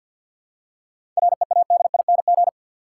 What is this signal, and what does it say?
Morse code, heard as a keyed tone, about 52 wpm, 700 Hz. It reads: READING
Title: Seven Letter Words - 50wpm